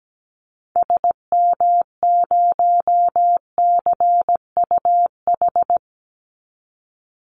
Morse code, heard as a keyed tone, about 17 wpm, 700 Hz. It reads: SM0CUH
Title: Call Signs Repeated 2x - 17wpm